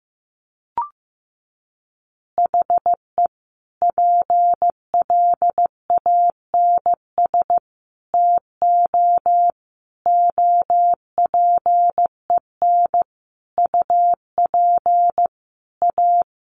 Morse code, heard as a keyed tone, about 15 wpm, 700 Hz. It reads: HE PLANS TO OPEN UP A
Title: Additional Easy Phrases 15wpm